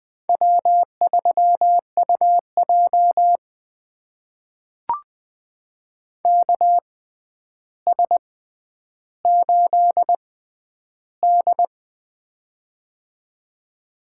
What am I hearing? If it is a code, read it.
Morse code, heard as a keyed tone, about 20 wpm, spaced as for 10 wpm, 700 Hz. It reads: W3UJ KS8D
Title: USA Call Signs - 4 Characters - 6x Character Spacing - 20wpm